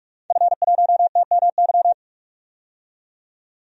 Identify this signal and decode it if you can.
Morse code, heard as a keyed tone, about 42 wpm, 700 Hz. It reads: F1TMY
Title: DXCC-100 Call Signs - 5 Characters - 45wpm